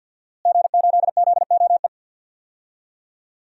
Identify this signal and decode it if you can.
Morse code, heard as a keyed tone, about 46 wpm, 700 Hz. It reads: G8ZOE
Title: DXCC-100 Call Signs - 5 Characters - Repeated 2x - 50wpm